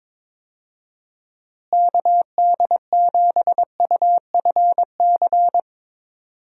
Morse code, heard as a keyed tone, about 22 wpm, 700 Hz. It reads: KD7UFC
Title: USA Call Signs - 6 Characters - Repeated 3x - 22wpm